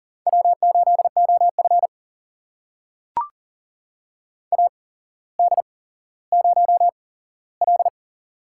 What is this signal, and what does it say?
Morse code, heard as a keyed tone, about 40 wpm, spaced as for 17 wpm, 700 Hz. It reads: W8OF AD0L
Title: USA Call Signs - 8x Character Spacing - 40wpm